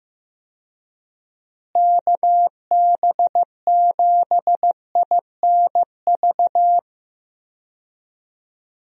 Morse code, heard as a keyed tone, about 15 wpm, 700 Hz. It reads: KB7INV
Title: USA Call Signs - 6 Characters - Repeated 2x - 15wpm